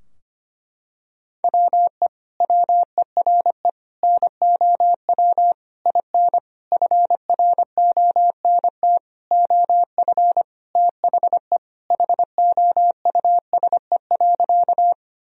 Morse code, heard as a keyed tone, about 25 wpm, 700 Hz. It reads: WE WERE NOW IN FRONT OF THE HOUSE.
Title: Sentences from Top 1000 Words - Review Part 2 - 25wpm